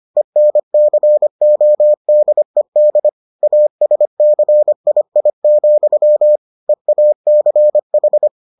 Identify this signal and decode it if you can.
Morse code, heard as a keyed tone, about 25 wpm, 600 Hz. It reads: ENCODED ASCII, EACH